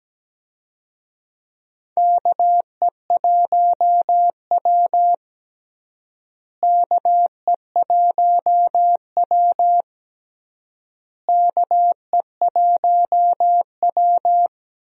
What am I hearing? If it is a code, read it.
Morse code, heard as a keyed tone, about 17 wpm, 700 Hz. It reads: KE1W KE1W KE1W
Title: USA Call Signs - 4 Characters - Repeated 3x - 17wpm